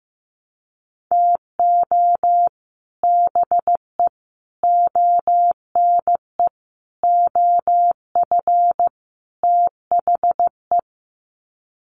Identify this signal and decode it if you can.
Morse code, heard as a keyed tone, about 15 wpm, 700 Hz. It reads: TO BE ONE OF THE
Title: Sets of 5 Words 15wpm